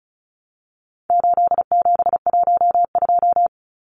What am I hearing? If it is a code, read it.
Morse code, heard as a keyed tone, about 34 wpm, 700 Hz. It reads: 8712
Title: Numbers - 4-Digit 35wpm